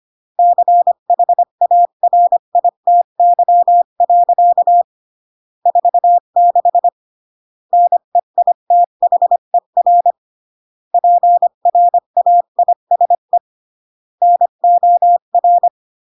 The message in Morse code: CHARITY. 46 NEITHER PRAISE NOR